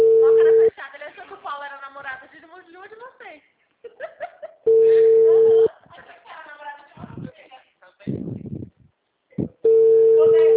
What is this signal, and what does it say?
Ringback tone (China, 450 Hz), with speech.
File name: out- -2014- - - .28